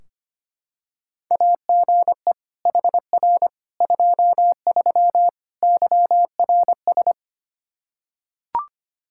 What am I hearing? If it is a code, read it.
Morse code, heard as a keyed tone, about 25 wpm, 700 Hz. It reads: AGE HR 23 YRS
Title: Practice QSO - Exchange 3 - More Info - 25wpm